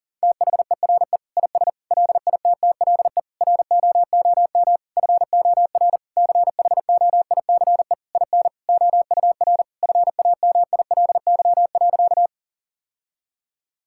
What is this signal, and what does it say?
Morse code, heard as a keyed tone, about 40 wpm, 700 Hz. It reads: THERE IS LITTLE ROOM FOR CHOICE IN OUR FAMILY.
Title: Sentences from Top 1000 Words - Review Part 6 - 40wpm